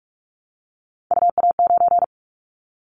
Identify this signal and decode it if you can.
Morse code, heard as a keyed tone, about 44 wpm, 700 Hz. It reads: UA9
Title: Callsign Prefixes for Top 25 DXCC - 45wpm